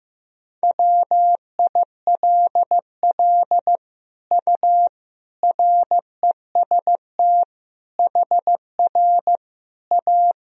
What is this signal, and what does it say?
Morse code, heard as a keyed tone, about 16 wpm, 700 Hz. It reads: WILL U REST HR A
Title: Sentences from Top 500 Words and Abbreviations 15wpm